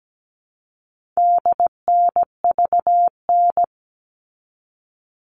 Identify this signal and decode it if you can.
Morse code, heard as a keyed tone, about 17 wpm, 700 Hz. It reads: DNVN
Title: All Cut Numbers - 4-Digit 17wpm